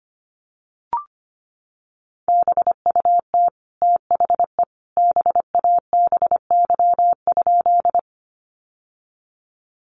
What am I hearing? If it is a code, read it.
Morse code, heard as a keyed tone, about 25 wpm, 700 Hz. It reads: BUT THE BABY?
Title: Sentences from Top 1300 Words 25wpm